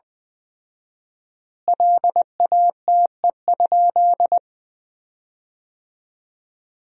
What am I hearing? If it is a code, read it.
Morse code, heard as a keyed tone, about 20 wpm, 700 Hz. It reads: LATE?
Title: 4x Word-Spacing - Sentences from Top 300 Words - 20wpm